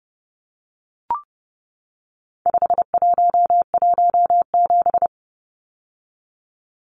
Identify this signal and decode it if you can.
Morse code, heard as a keyed tone, about 30 wpm, 700 Hz. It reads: 5117
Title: Numbers - 4-Digit 30wpm